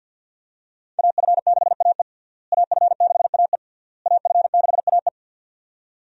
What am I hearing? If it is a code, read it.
Morse code, heard as a keyed tone, about 51 wpm, 700 Hz. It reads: AF6AE AF6AE AF6AE
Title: USA Call Signs - 5 Characters - Repeated 3x - 50wpm